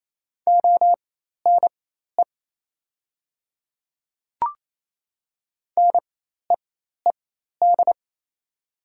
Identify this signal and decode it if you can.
Morse code, heard as a keyed tone, about 27 wpm, spaced as for 13 wpm, 700 Hz. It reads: ONE NEED
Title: 4x Character Spacing - Top 200 Words - 28wpm